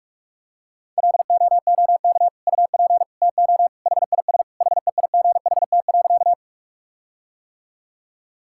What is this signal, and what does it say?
Morse code, heard as a keyed tone, about 46 wpm, 700 Hz. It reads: LOOK UP TO HIS HEIGHT.
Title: Sentences from Top 1600 Words 45wpm